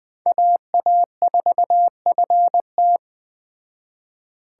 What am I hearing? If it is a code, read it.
Morse code, heard as a keyed tone, about 20 wpm, 700 Hz. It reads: AA4FT